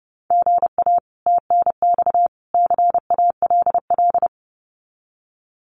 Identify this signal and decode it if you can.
Morse code, heard as a keyed tone, about 30 wpm, 700 Hz. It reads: GA TNX CALL
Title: Common QSO Phrases 30wpm